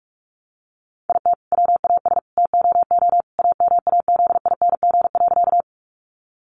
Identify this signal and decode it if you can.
Morse code, heard as a keyed tone, about 44 wpm, 700 Hz. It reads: IT WAS TOO AMAZING.
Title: Sentences from Top 1500 Words 45wpm